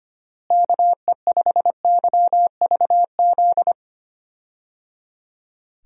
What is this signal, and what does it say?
Morse code, heard as a keyed tone, about 25 wpm, 700 Hz. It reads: KE5YVZ
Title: USA Call Signs - 6 Characters - Repeated 3x - 25wpm